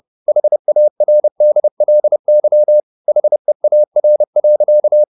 Morse code, heard as a keyed tone, about 30 wpm, 600 Hz. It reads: HARDLY HEAR.